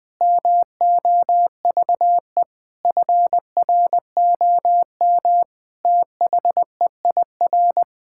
Morse code, heard as a keyed tone, about 20 wpm, 700 Hz. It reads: MOVE FROM THEIR
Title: Sentences from Top 1000 Words - Review Part 6 - 20wpm